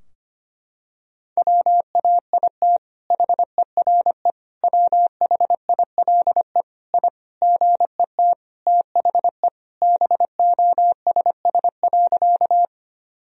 Morse code, heard as a keyed tone, about 25 wpm, 700 Hz. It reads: WAIT HERE WHILE I GET THE BOSS.
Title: Sentences from Top 1000 Words - Review Part 4 - 25wpm